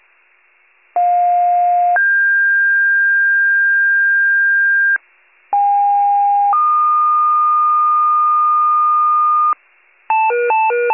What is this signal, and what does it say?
Two-tone page, heard as a one-second tone, then a three-second tone, about 710 then 1600 Hz, about 810 then 1220 Hz, with noise. It is heard from 1.0 to 9.5 s.